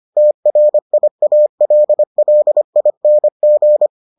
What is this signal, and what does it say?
Morse code, heard as a keyed tone, about 25 wpm, 600 Hz. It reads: TRIALLING